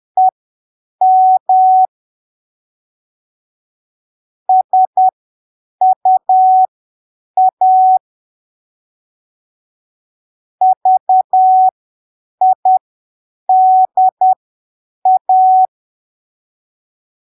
Morse code, heard as a keyed tone, about 10 wpm, spaced as for 7 wpm, 750 Hz. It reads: EM SUA VIDA
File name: arquivos / texto / texto 28 10ppm